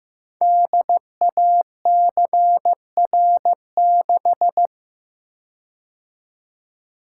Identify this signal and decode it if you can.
Morse code, heard as a keyed tone, about 15 wpm, 700 Hz. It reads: DACR6
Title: Code Groups - 5 Letter:Numbers - 15wpm